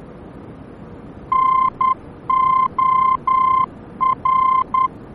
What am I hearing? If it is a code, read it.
Morse code, heard as a keyed tone, about 10 wpm, 1040 Hz. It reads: NOR